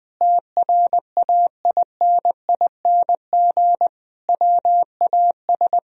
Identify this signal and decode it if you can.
Morse code, heard as a keyed tone, about 20 wpm, 700 Hz. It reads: TRAINING WAS
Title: Sentences from Top 1000 Words - Review Part 4 - 20wpm